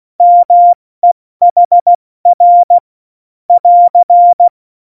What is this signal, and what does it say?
Morse code, heard as a keyed tone, about 16 wpm, spaced as for 14 wpm, 700 Hz. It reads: MEHR <AR>
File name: Morsekurs / Witze / Witz 12. Juni 2018, 70 F80